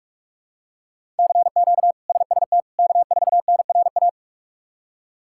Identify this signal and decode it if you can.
Morse code, heard as a keyed tone, about 46 wpm, 700 Hz. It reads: CQ SST K4NRA
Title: K1USN SST QSO Practice - 45wpm